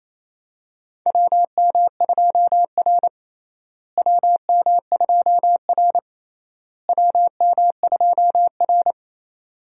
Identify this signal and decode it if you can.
Morse code, heard as a keyed tone, about 28 wpm, 700 Hz. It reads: WM2R WM2R WM2R
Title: USA Call Signs - 4 Characters - Repeated 3x - 28wpm